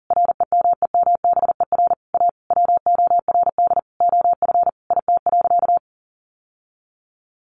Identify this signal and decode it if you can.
Morse code, heard as a keyed tone, about 37 wpm, 700 Hz. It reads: REMEMBER A WORD OF IT.
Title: Sentences from Top 1200 Words 40wpm